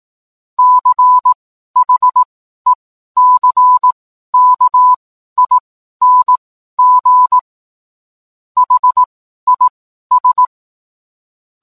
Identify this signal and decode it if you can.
Morse code, heard as a keyed tone, about 18 wpm, spaced as for 12 wpm, 1000 Hz. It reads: CHECKING HIS